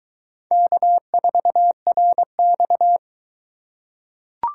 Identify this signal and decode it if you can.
Morse code, heard as a keyed tone, about 22 wpm, 700 Hz. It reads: K4RX